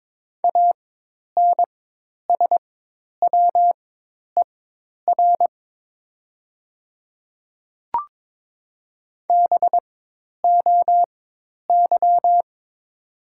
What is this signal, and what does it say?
Morse code, heard as a keyed tone, about 22 wpm, spaced as for 10 wpm, 700 Hz. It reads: ANSWER BOY
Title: 4x Character Spacing - Top 500 Words - 22wpm